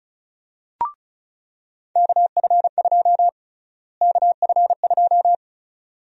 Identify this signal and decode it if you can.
Morse code, heard as a keyed tone, about 34 wpm, 700 Hz. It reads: KF2 KF2
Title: USA Call Sign Prefixes - Repeated 2x - 35wpm